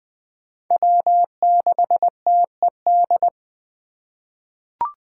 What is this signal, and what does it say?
Morse code, heard as a keyed tone, about 21 wpm, 700 Hz. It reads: W6TED
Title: 2023 SOTA Chaser Call Signs - 20wpm